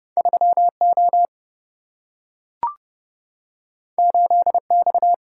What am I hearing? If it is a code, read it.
Morse code, heard as a keyed tone, about 30 wpm, 700 Hz. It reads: 3O 8X
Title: Single-Letter Single-Number 30wpm